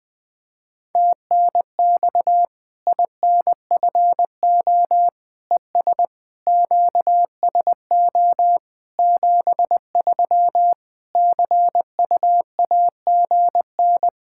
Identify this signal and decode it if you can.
Morse code, heard as a keyed tone, about 20 wpm, 700 Hz. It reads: TNX INFO ES QSO 73 CUAGN
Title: K1USN SST QSO Practice 20wpm